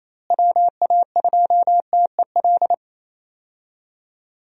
Morse code, heard as a keyed tone, about 28 wpm, 700 Hz. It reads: WA2TEL
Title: USA Call Signs - 6 Characters - Repeated 2x - 28wpm